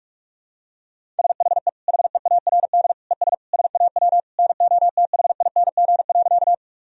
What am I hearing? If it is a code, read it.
Morse code, heard as a keyed tone, about 48 wpm, 700 Hz. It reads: SHE HEARD ES SAW NOTHING.
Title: Sentences from Top 500 Words and Abbreviations 45wpm